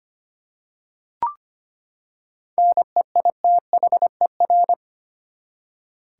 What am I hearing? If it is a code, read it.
Morse code, heard as a keyed tone, about 25 wpm, 700 Hz. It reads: NEITHER
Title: Seven Letter Words - 25wpm